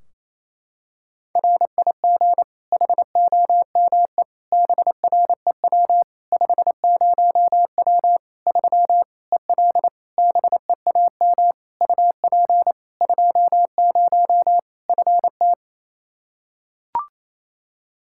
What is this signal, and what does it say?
Morse code, heard as a keyed tone, about 28 wpm, 700 Hz. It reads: RIG HOME BREW 50W 3 EL BEAM UP 20 FT
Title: Practice QSO - Exchange 2 - Info - 28wpm